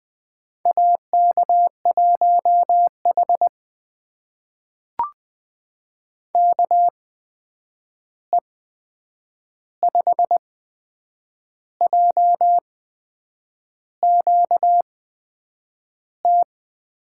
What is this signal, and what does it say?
Morse code, heard as a keyed tone, about 20 wpm, spaced as for 7 wpm, 700 Hz. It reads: AK1H KE5JQT
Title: USA Call Signs - 8x Character Spacing - 20wpm